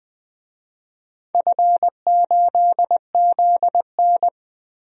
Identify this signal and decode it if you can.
Morse code, heard as a keyed tone, about 19 wpm, 700 Hz. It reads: F8ZN
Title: DXCC-100 Call Signs - 4 Characters - Repeated 3x - 20wpm